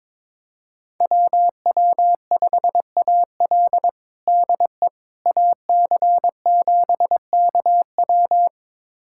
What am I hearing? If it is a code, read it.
Morse code, heard as a keyed tone, about 22 wpm, 700 Hz. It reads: WW5AL DE AC7KW
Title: USA Callsign DE Callsign - 5 Characters - 22wpm